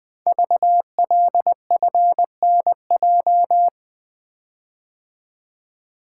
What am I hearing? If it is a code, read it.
Morse code, heard as a keyed tone, about 20 wpm, 700 Hz. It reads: VLFNJ
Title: Code Groups - 5 Letters - 20wpm